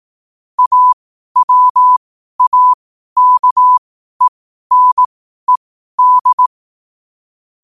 Morse code, heard as a keyed tone, about 18 wpm, spaced as for 12 wpm, 1000 Hz. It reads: AWAKENED
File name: qotdfiles / 12wpm